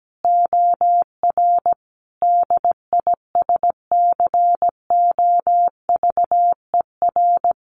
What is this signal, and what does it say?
Morse code, heard as a keyed tone, about 17 wpm, 700 Hz. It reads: OR DISCOVER
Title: Sentences from Top 1100 Words 17wpm